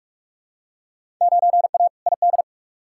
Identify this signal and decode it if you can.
Morse code, heard as a keyed tone, about 46 wpm, 700 Hz. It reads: 9A ID
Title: Field Day Exchanges Encore - 45wpm